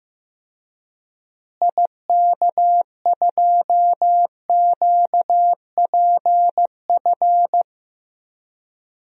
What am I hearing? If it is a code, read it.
Morse code, heard as a keyed tone, about 15 wpm, 700 Hz. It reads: IK2QPF
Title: DXCC-100 Call Signs - 6 Characters - Repeated 2x - 15wpm